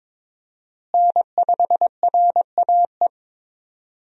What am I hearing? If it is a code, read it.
Morse code, heard as a keyed tone, about 22 wpm, 700 Hz. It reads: N5RAE
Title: USA Call Signs - 5 Characters - Repeated 3x - 22wpm